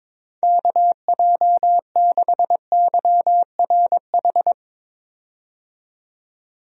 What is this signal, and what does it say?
Morse code, heard as a keyed tone, about 22 wpm, 700 Hz. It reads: KJ6YRH
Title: USA Call Signs - 6 Characters - 22wpm